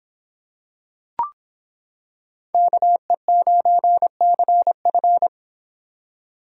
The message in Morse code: KE9CF

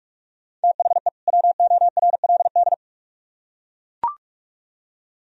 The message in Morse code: THE WORLD

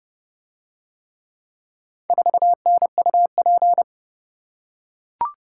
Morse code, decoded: 4NUP

